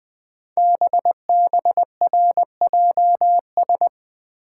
Morse code, heard as a keyed tone, about 20 wpm, 700 Hz. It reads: BBRJS
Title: Code Groups - 5 Letters - 20wpm